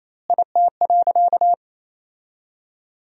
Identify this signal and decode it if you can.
Morse code, heard as a keyed tone, about 28 wpm, 700 Hz. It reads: IT.
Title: 2x Word-Spacing - Sentences from Top 300 Words - 28wpm